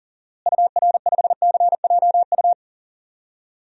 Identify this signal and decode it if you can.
Morse code, heard as a keyed tone, about 40 wpm, 700 Hz. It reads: UR5CJU
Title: DXCC-100 Call Signs - 6 Characters - Repeated 2x - 40wpm